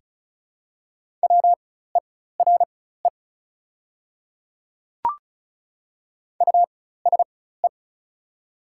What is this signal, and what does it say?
Morse code, heard as a keyed tone, about 36 wpm, spaced as for 16 wpm, 700 Hz. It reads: WERE USE
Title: 4x Character Spacing - Top 100 Words - 35wpm